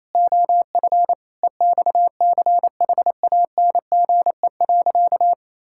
Morse code, heard as a keyed tone, about 27 wpm, 700 Hz. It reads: OF EXCHANGE.
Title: Sentences from Top 1000 Words - Review Part 6 - 28wpm